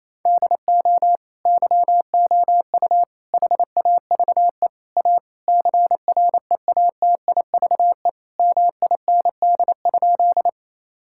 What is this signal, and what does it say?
Morse code, heard as a keyed tone, about 28 wpm, 700 Hz. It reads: DO YOU HAVE A CREATIVE MIND?
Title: Sentences from Top 1000 Words - Review Part 5 - 28wpm